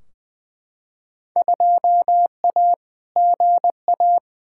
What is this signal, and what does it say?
Morse code, heard as a keyed tone, about 20 wpm, 700 Hz. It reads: 2A GA
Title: Field Day Exchanges Encore - 20wpm